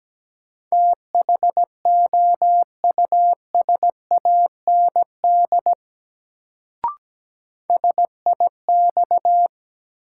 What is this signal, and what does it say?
Morse code, heard as a keyed tone, about 17 wpm, 700 Hz. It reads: THOUSAND SIX
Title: Sentences from Top 600 Words - New Word Drill - 17wpm